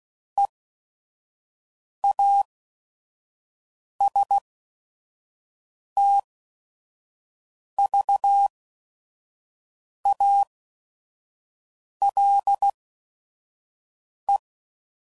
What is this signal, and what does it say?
Morse code, heard as a keyed tone, about 16 wpm, spaced as for 5 wpm, 800 Hz. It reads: EASTVALE
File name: CW Code Groups @5WPM / fromSM's / Cities